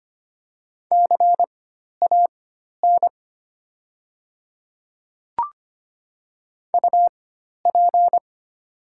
Morse code, heard as a keyed tone, about 25 wpm, spaced as for 12 wpm, 700 Hz. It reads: CAN UP
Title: 4x Character Spacing - Top 100 Words - 25wpm